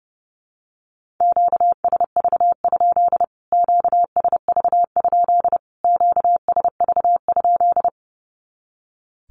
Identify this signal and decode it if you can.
Morse code, heard as a keyed tone, about 30 wpm, 700 Hz. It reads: QSV? QSV? QSV?
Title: Q-Codes Repeated 3x 30wpm